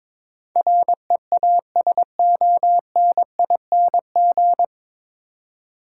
Morse code, heard as a keyed tone, about 22 wpm, 700 Hz. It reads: REASONING
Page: Morse Code Ninja practice 22wpm